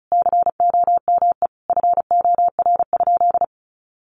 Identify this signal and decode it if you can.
Morse code, heard as a keyed tone, about 34 wpm, 700 Hz. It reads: COME FOR?
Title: Sentences from Top 100 Words 35wpm